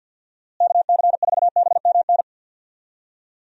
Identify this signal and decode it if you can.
Morse code, heard as a keyed tone, about 47 wpm, 700 Hz. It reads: KC4BMN
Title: USA Call Signs - 6 Characters - Repeated 3x - 50wpm